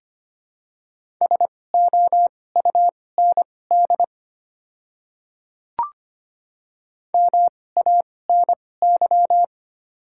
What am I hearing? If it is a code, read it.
Morse code, heard as a keyed tone, about 25 wpm, spaced as for 18 wpm, 700 Hz. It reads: SOUND MANY
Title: 2x Character Spacing - Top 100 Words - 25wpm